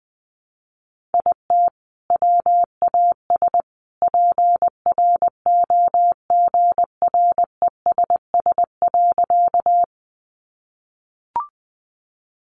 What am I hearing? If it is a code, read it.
Morse code, heard as a keyed tone, about 20 wpm, 700 Hz. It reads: IT WAS PROGRESS.
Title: Sentences from Top 1400 Words 20wpm